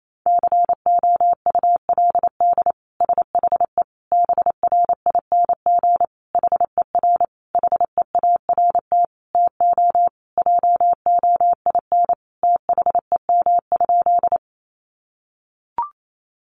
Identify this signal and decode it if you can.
Morse code, heard as a keyed tone, about 28 wpm, 700 Hz. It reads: COULD SHE BRING HER HEART TO JOIN THEM?
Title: Sentences from Top 1400 Words 28wpm